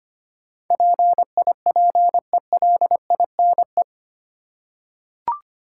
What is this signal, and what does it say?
Morse code, heard as a keyed tone, about 25 wpm, 700 Hz. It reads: PIPELINE
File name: Eight Letter Words - 25wpm